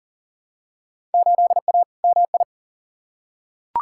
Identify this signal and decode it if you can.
Morse code, heard as a keyed tone, about 40 wpm, 700 Hz. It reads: 8A MI
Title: Field Day Exchanges Encore - 40wpm